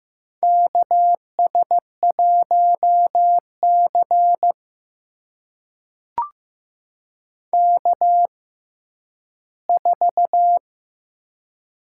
Morse code, heard as a keyed tone, about 15 wpm, spaced as for 10 wpm, 700 Hz. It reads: KS1C K4
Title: USA Call Signs - 6x Character Spacing - 15wpm